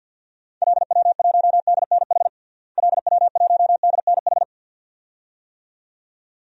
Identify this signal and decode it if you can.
Morse code, heard as a keyed tone, about 47 wpm, 700 Hz. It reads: LW1DNH LW1DNH
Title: DXCC-100 Call Signs - 6 Characters - Repeated 2x - 50wpm